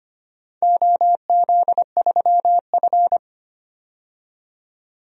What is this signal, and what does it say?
Morse code, heard as a keyed tone, about 25 wpm, 700 Hz. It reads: OZ3F
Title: DXCC-100 Call Signs - 4 Characters - 25wpm